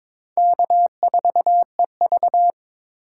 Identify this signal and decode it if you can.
Morse code, heard as a keyed tone, about 22 wpm, 700 Hz. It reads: K4EV
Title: USA Call Signs - 4 Characters - 3x Character Spacing - 22wpm